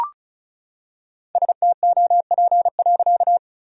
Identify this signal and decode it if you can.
Morse code, heard as a keyed tone, about 36 wpm, 700 Hz. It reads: STOP.
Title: Sentences from Top 1000 Words - Review Part 1 - 35wpm